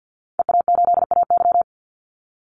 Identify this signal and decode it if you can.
Morse code, heard as a keyed tone, about 50 wpm, 700 Hz. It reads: EA8AY
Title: Call Signs Repeated 3x - 50wpm